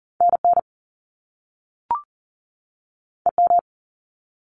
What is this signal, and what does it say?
Morse code, heard as a keyed tone, about 40 wpm, 700 Hz. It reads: NN EM
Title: Top Two Letter Combinations in Words 40wpm